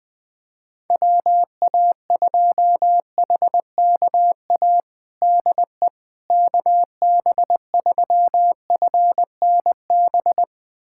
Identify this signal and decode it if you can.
Morse code, heard as a keyed tone, about 20 wpm, 700 Hz. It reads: WA2HKA DE KB3FNB
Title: USA Callsign DE Callsign - 6 Characters - 20wpm